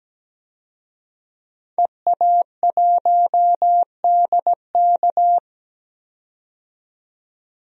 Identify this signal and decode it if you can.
Morse code, heard as a keyed tone, about 17 wpm, 700 Hz. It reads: EA1DK
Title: DXCC-100 Call Signs - 5 Characters - 17wpm